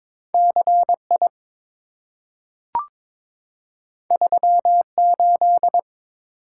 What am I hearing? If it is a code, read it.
Morse code, heard as a keyed tone, about 22 wpm, 690 Hz. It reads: CI 38
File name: practice / 22wpm / Two Letter-Number Combinations - 22wpm